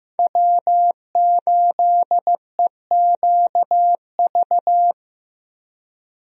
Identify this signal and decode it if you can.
Morse code, heard as a keyed tone, about 15 wpm, 700 Hz. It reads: W8EQV
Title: USA Call Signs - 5 Characters - Repeated 3x - 15wpm